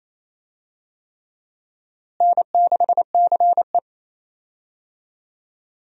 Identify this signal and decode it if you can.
Morse code, heard as a keyed tone, about 28 wpm, 700 Hz. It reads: N6CE
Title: USA Call Signs - 4 Characters - Repeated 3x - 28wpm